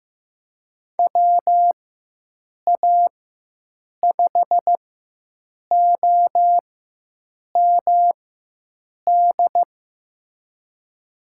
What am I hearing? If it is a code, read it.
Morse code, heard as a keyed tone, about 15 wpm, spaced as for 7 wpm, 700 Hz. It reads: WA5OMD
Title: USA Call Signs - 4x Character Spacing - 15wpm